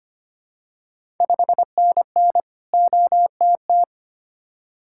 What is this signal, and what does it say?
Morse code, heard as a keyed tone, about 25 wpm, 700 Hz. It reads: 5NN OTT
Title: Ontario QSO Party Exchange 25wpm